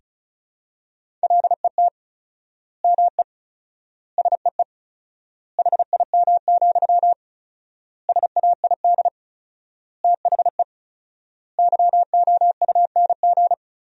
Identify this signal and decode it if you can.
Morse code, heard as a keyed tone, about 36 wpm, 700 Hz. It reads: LET ME SEE HIM, SAID THE YOUNG
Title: 4x Word-Spacing - Sentences from Top 400 Words - 35wpm